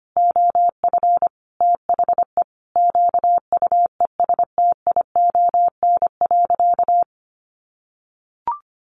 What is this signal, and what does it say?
Morse code, heard as a keyed tone, about 25 wpm, 700 Hz. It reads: OF THE QUESTION.
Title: Sentences from Top 600 Words 25wpm